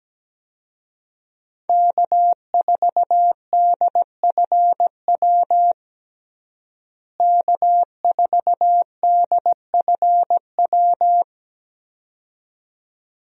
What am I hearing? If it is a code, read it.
Morse code, heard as a keyed tone, about 17 wpm, 700 Hz. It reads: K4DFW K4DFW
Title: USA Call Signs - 5 Characters - Repeated 2x - 17wpm